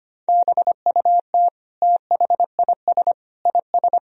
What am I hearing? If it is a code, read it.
Morse code, heard as a keyed tone, about 25 wpm, 700 Hz. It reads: BUT THIS IS
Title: Sentences from Top 1000 Words - Review Part 2 - 25wpm